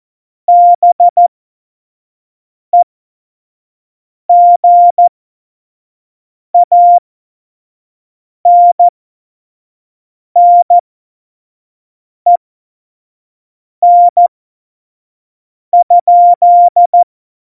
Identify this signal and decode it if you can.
Morse code, heard as a keyed tone, about 14 wpm, spaced as for 5 wpm, 700 Hz. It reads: BEGANNEN?
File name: Morsekurs / Witze / Witz 29. Dezember 2020, 25 F70